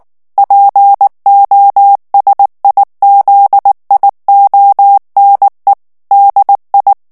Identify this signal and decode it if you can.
Morse code, heard as a keyed tone, about 19 wpm, 800 Hz. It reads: POSIZIONE DI
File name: morse mp3 / ITALIEN texte clair 2 vitesse 18 mots minute